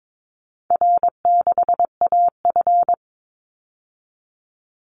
Morse code, heard as a keyed tone, about 22 wpm, 700 Hz. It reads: R6AF
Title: Call Signs Repeated 2x - 22wpm